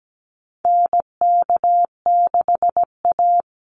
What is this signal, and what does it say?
Morse code, heard as a keyed tone, about 17 wpm, 700 Hz. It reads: NK6A
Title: NA SOTA Chaser Callsigns 17wpm